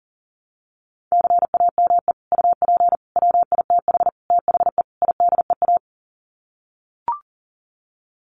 Morse code, heard as a keyed tone, about 40 wpm, 700 Hz. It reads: CAME UP WITH THE IDEA
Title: Sets of 5 Words Encore2 40wpm